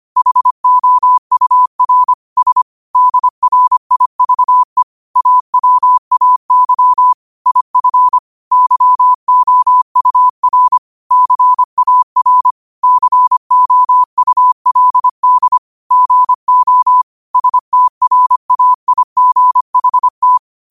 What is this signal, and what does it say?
Morse code, heard as a keyed tone, about 25 wpm, 1000 Hz. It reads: HOUR S DRIVE AWAY IF YOUR CAR COULD GO STRAIGHT
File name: qotdfiles / 25wpm